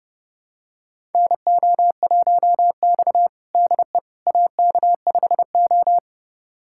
Morse code, heard as a keyed tone, about 32 wpm, 700 Hz. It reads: NO1X DE AK5O
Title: USA Callsign DE Callsign - 4 Characters - 30wpm